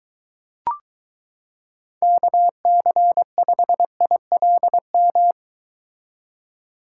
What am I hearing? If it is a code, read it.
Morse code, heard as a keyed tone, about 22 wpm, 700 Hz. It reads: KC5ILM